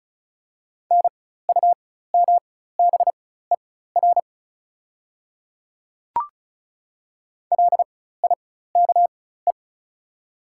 Morse code, heard as a keyed tone, about 34 wpm, spaced as for 16 wpm, 700 Hz. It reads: NUMBER LIKE
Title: 4x Character Spacing - Top 200 Words - 35wpm